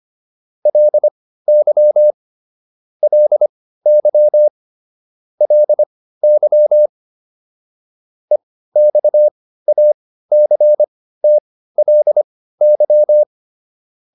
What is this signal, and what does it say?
Morse code, heard as a keyed tone, about 25 wpm, spaced as for 15 wpm, 600 Hz. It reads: LY LY LY EXACTLY